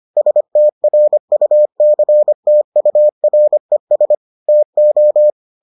Morse code, heard as a keyed tone, about 25 wpm, 600 Hz. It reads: STRUCTURES TO